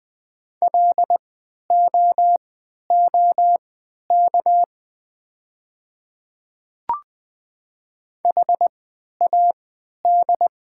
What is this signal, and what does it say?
Morse code, heard as a keyed tone, about 20 wpm, spaced as for 11 wpm, 700 Hz. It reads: LOOK HAD
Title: 3x Character Spacing - Top 100 Words - 20wpm